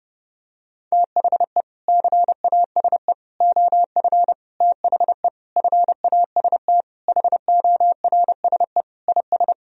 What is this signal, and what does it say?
Morse code, heard as a keyed tone, about 30 wpm, 700 Hz. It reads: THE CASE OF THE FAST HORSE IS
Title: Sentences from Top 1000 Words - Review Part 2 - 30wpm